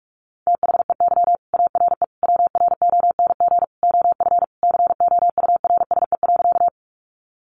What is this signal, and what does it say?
Morse code, heard as a keyed tone, about 44 wpm, 700 Hz. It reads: THEY ARE WRONG OF COURSE.
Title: Sentences from Top 900 Words 45wpm